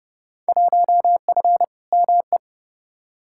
Morse code, heard as a keyed tone, about 30 wpm, 700 Hz. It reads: 1F ME
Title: Field Day Exchanges Encore - 30wpm